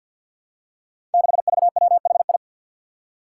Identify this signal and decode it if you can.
Morse code, heard as a keyed tone, about 47 wpm, 700 Hz. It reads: 6VWHI
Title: Five Challenging Letter-Number Combinations 50wpm